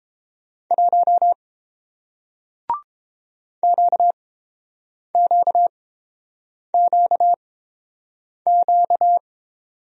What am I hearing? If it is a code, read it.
Morse code, heard as a keyed tone, about 30 wpm, spaced as for 7 wpm, 700 Hz. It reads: 1 QQQQ